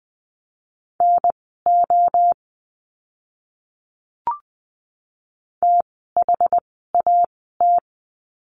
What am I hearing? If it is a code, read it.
Morse code, heard as a keyed tone, about 20 wpm, spaced as for 15 wpm, 700 Hz. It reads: NO THAT
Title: 061 Lesson 11f - Letter h - Words 20wpm